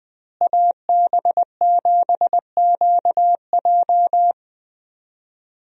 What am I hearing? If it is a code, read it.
Morse code, heard as a keyed tone, about 20 wpm, 700 Hz. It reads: AB7QJ
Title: USA Call Signs - 5 Characters - Repeated 2x - 20wpm